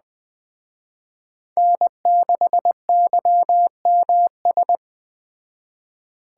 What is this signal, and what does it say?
Morse code, heard as a keyed tone, about 20 wpm, 700 Hz. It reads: N6YMS
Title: Speed Racing - USA 1x3 Call Signs 17wpm